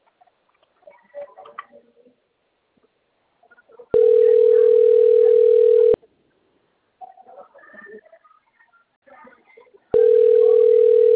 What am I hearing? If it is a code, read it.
Ringback tone (North America), 440 + 480 Hz.